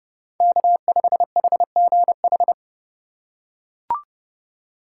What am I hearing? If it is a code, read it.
Morse code, heard as a keyed tone, about 29 wpm, 700 Hz. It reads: K5HGH